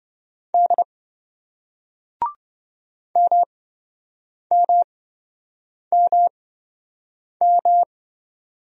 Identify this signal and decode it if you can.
Morse code, heard as a keyed tone, about 27 wpm, spaced as for 7 wpm, 700 Hz. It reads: D MMMM